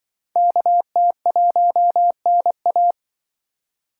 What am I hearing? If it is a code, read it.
Morse code, heard as a keyed tone, about 25 wpm, 700 Hz. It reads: KT1NA
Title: Speed Racing - USA 2x2 Call Signs 20wpm